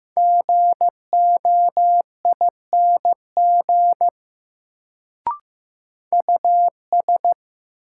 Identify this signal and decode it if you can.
Morse code, heard as a keyed tone, about 15 wpm, 700 Hz. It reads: GOING US
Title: Sets of 2 Words - New Word Drill - 15wpm